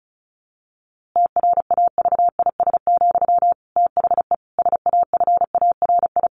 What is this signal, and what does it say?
Morse code, heard as a keyed tone, about 34 wpm, 700 Hz. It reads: TRAVIS, THE SAFARI
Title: A Sound Of Thunder 35wpm